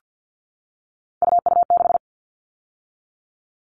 Morse code, heard as a keyed tone, about 50 wpm, 700 Hz. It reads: UU6
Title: All Cut Numbers - 3-Digit 50wpm